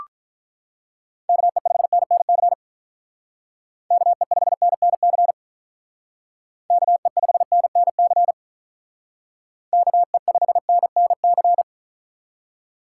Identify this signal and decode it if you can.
Morse code, heard as a keyed tone, about 43 wpm, 700 Hz. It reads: KE5NNC KE5NNC KE5NNC KE5NNC
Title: Speed Racing - USA 2x3 Call Signs 35wpm